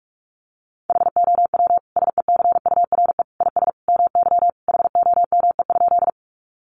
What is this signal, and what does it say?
Morse code, heard as a keyed tone, about 44 wpm, 700 Hz. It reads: HOW SECURE IS MY HOME?
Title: Sentences from Top 1200 Words 45wpm